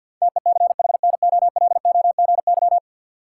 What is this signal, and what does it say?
Morse code, heard as a keyed tone, about 53 wpm, 700 Hz. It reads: TECHNOLOGY
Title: Sentences from Top 1000 Words - Review Part 5 - New Word Drill - 50wpm